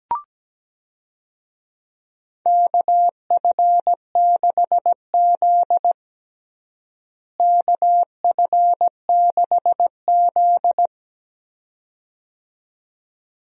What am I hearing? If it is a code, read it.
Morse code, heard as a keyed tone, about 17 wpm, 700 Hz. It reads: KF6Z KF6Z
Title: USA Call Signs - 4 Characters - Repeated 2x - 17wpm